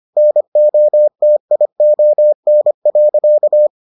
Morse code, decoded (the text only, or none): NOTION.